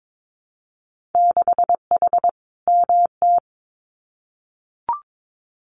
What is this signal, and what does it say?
Morse code, heard as a keyed tone, about 22 wpm, 700 Hz. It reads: 6H MT
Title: Winter Field Day Exchanges Encore 22wpm